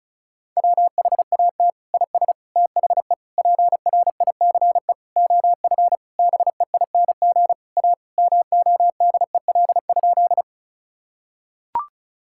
Morse code, heard as a keyed tone, about 36 wpm, 700 Hz. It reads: WHAT IS THE PRICE OF BEING A MODEL?
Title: Sentences from Top 1000 Words - Review Part 5 - 35wpm